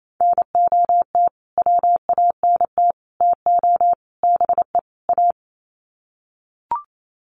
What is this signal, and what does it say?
Morse code, heard as a keyed tone, about 28 wpm, 700 Hz. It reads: NOT WANT TO BE A
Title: Sets of 5 Words Encore 28wpm